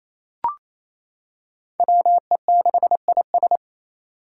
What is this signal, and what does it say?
Morse code, heard as a keyed tone, about 28 wpm, 700 Hz. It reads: WE6IS
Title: Five Challenging Letter-Number Combinations 28wpm